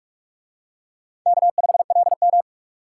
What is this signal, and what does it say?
Morse code, heard as a keyed tone, about 41 wpm, 700 Hz. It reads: K5LM